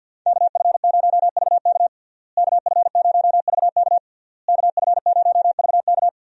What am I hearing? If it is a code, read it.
Morse code, heard as a keyed tone, about 50 wpm, 700 Hz. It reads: KF0VK KF0VK KF0VK
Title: USA Call Signs - 5 Characters - Repeated 3x - 50wpm